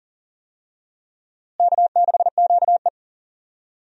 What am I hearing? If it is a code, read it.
Morse code, heard as a keyed tone, about 40 wpm, 700 Hz. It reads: K6QE
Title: USA Call Signs - 4 Characters - Repeated 3x - 40wpm